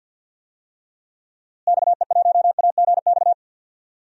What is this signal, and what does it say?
Morse code, heard as a keyed tone, about 53 wpm, 700 Hz. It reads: XE1AGX
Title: DXCC-100 Call Signs - 6 Characters - Repeated 2x - 50wpm